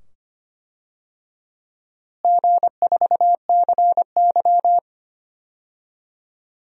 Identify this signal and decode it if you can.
Morse code, heard as a keyed tone, about 25 wpm, 700 Hz. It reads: G4CY
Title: DXCC-100 Call Signs - 4 Characters - Repeated 2x - 25wpm